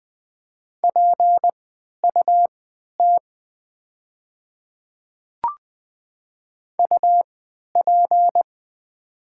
Morse code, heard as a keyed tone, about 20 wpm, spaced as for 11 wpm, 700 Hz. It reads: PUT UP
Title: 3x Character Spacing - Top 200 Words - 20wpm